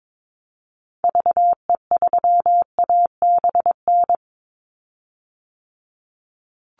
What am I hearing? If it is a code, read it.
Morse code, heard as a keyed tone, about 22 wpm, 700 Hz. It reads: VE3ABN
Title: Call Signs Repeated 2x - 22wpm